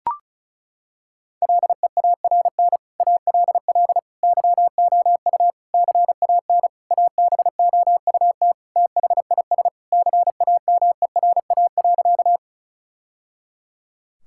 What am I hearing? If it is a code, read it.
Morse code, heard as a keyed tone, about 34 wpm, 700 Hz. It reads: LEARN ALL YOU CAN ABOUT THIS CAMERA.
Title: Sentences from Top 1000 Words - Review Part 5 - 35wpm